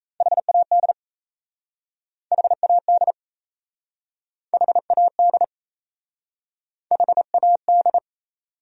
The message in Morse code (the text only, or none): HAD HAD HAD HAD